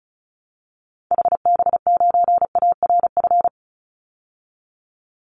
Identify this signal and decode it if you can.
Morse code, heard as a keyed tone, about 34 wpm, 700 Hz. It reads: HB9ARF
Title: Call Signs - 35wpm